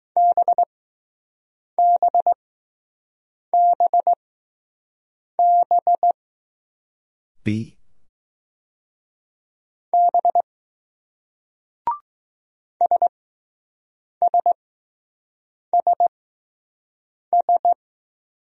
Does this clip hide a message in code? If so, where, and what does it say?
Yes: Morse code from 0.2 to 17.7 s, heard as a keyed tone, about 20 wpm, spaced as for 6 wpm, 700 Hz. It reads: BBBB B SSSS